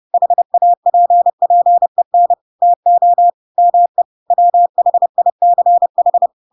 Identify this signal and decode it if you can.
Morse code, heard as a keyed tone, about 30 wpm, 700 Hz. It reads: HAPPEN TO ME WHICH
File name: Pinocchio / Pinocchio ch4 30 30 ik1hge